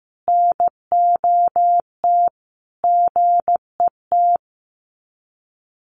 Morse code, heard as a keyed tone, about 15 wpm, 700 Hz. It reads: NOT GET
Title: Sets of 5 Words Encore2 15wpm